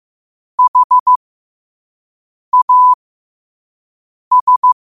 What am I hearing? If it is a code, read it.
Morse code, heard as a keyed tone, about 15 wpm, spaced as for 5 wpm, 1000 Hz. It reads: HAS
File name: qotdfiles / 5wpm